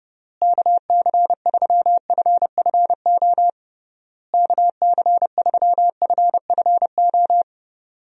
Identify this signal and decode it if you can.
Morse code, heard as a keyed tone, about 32 wpm, 700 Hz. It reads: KC3FFO KC3FFO
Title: USA Call Signs - 6 Characters - Repeated 3x - 30wpm